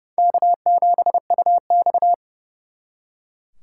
Morse code, heard as a keyed tone, about 36 wpm, 700 Hz. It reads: K7UX
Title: Speed Racing - USA 1x2 Call Signs 30wpm